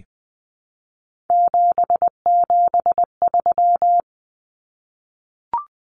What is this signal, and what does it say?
Morse code, heard as a keyed tone, about 20 wpm, 700 Hz. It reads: 773
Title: Common Cut Numbers - 3-Digit 20wpm